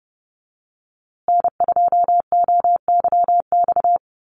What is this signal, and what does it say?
Morse code, heard as a keyed tone, about 30 wpm, 700 Hz. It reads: N2OYX
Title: USA Call Signs Repeated 3x 30wpm